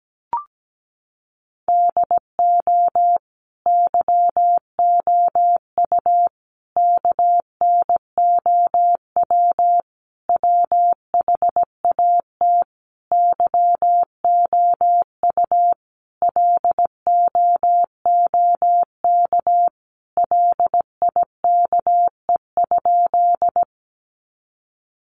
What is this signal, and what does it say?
Morse code, heard as a keyed tone, about 17 wpm, 700 Hz. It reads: DO YOU KNOW WHAT YOU LOOK LIKE?
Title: Sentences from Top 100 Words 17wpm